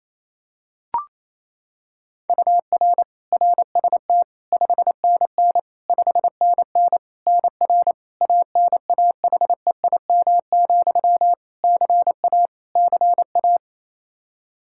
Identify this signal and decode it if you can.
Morse code, heard as a keyed tone, about 27 wpm, 700 Hz. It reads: UR RST 5NN 5NN NR ANAHEIM, CA CA
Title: Practice QSO - Exchange 1 - Beginning - 28wpm